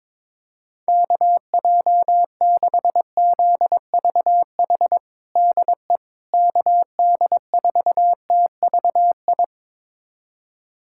Morse code, heard as a keyed tone, about 22 wpm, 700 Hz. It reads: KJ6ZVH DE KD4TVI
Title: USA Callsign DE Callsign 22wpm